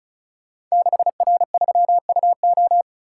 Morse code, heard as a keyed tone, about 36 wpm, 700 Hz. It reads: 6R3UO
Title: Code Groups - 5 Letter:Numbers - 35wpm